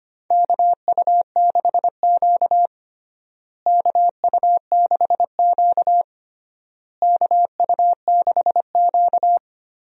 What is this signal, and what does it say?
Morse code, heard as a keyed tone, about 25 wpm, 700 Hz. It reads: KU6Q KU6Q KU6Q
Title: USA Call Signs - 4 Characters - Repeated 3x - 25wpm